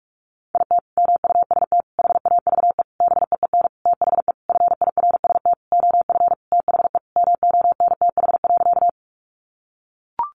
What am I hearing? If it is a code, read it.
Morse code, heard as a keyed tone, about 44 wpm, 700 Hz. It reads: IT MUST HAVE BEEN THE FIRST OF THE MONTH.
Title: Sentences from Top 800 Words 45wpm